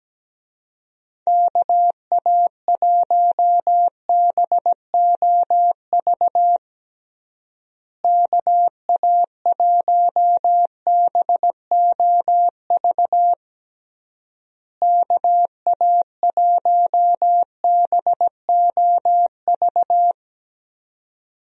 Morse code, heard as a keyed tone, about 17 wpm, 700 Hz. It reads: KA1BOV KA1BOV KA1BOV
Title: USA Call Signs - 6 Characters - Repeated 3x - 17wpm